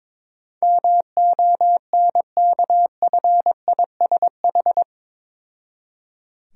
Morse code, heard as a keyed tone, about 22 wpm, 690 Hz. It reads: MONKFISH
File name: Difficult Words with Hints - 22wpm